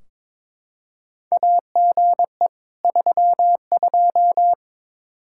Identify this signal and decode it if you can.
Morse code, heard as a keyed tone, about 22 wpm, 700 Hz. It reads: AGE 32
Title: K1USN SST QSO Practice 22wpm